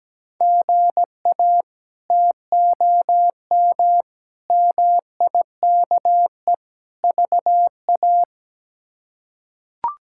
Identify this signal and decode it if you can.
Morse code, heard as a keyed tone, about 17 wpm, 700 Hz. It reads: GA TOM MIKE VA
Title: K1USN SST QSO Practice - 17wpm